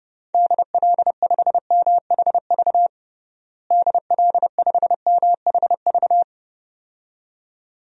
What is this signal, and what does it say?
Morse code, heard as a keyed tone, about 30 wpm, 700 Hz. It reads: DL5MHV DL5MHV
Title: DXCC-100 Call Signs - 6 Characters - Repeated 2x - 30wpm